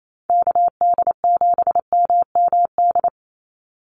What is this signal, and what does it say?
Morse code, heard as a keyed tone, about 28 wpm, 700 Hz. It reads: KD7MMD
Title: USA Call Signs 28wpm